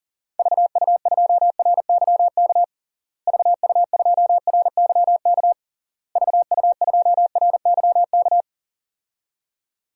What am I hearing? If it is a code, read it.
Morse code, heard as a keyed tone, about 40 wpm, 700 Hz. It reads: VU2RYK VU2RYK VU2RYK
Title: DXCC-100 Call Signs - 6 Characters - Repeated 3x - 40wpm